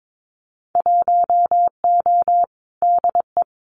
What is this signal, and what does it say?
Morse code, heard as a keyed tone, about 22 wpm, 700 Hz. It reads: 1O DE
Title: Winter Field Day Exchanges 22wpm